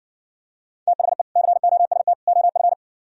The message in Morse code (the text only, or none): THE COST OF